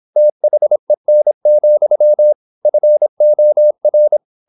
Morse code, heard as a keyed tone, about 26 wpm, 600 Hz. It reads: THEN, FOR
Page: Never 26wpm | CW med Gnister